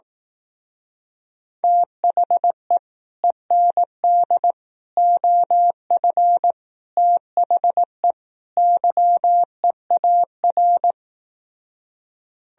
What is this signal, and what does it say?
Morse code, heard as a keyed tone, about 18 wpm, 700 Hz. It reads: THE END OF THE YEAR
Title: Speed-Racing - Sets of 5 Words 15wpm